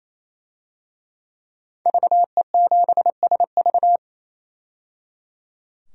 Morse code, heard as a keyed tone, about 27 wpm, 700 Hz. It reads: VE7SV
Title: DXCC-100 Call Signs - 5 Characters - 28wpm